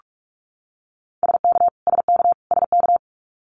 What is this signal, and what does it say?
Morse code, heard as a keyed tone, about 44 wpm, 700 Hz. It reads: SK SK SK
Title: Canadian Provinces and Territories - Repeated 3x - 45wpm